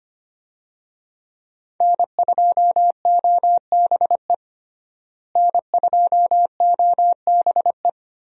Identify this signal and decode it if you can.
Morse code, heard as a keyed tone, about 25 wpm, 700 Hz. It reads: N2OBE N2OBE
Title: USA Call Signs - 5 Characters - Repeated 3x - 25wpm